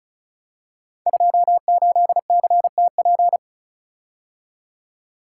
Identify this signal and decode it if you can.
Morse code, heard as a keyed tone, about 35 wpm, 700 Hz. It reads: 28CTP
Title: Code Groups - 5 Letter:Numbers - 35wpm